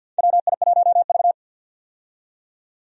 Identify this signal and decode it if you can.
Morse code, heard as a keyed tone, about 53 wpm, 700 Hz. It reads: WI1V
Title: USA Call Signs - 4 Characters - 50wpm